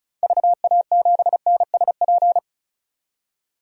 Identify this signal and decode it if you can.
Morse code, heard as a keyed tone, about 34 wpm, 700 Hz. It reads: VA7NSP
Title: DXCC-100 Call Signs - 6 Characters - 35wpm